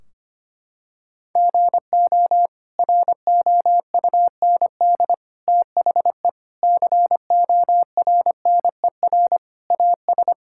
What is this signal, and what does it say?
Morse code, heard as a keyed tone, about 25 wpm, 700 Hz. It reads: GO ROUND THE CORNER AS
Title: Sentences from Top 1600 Words 25wpm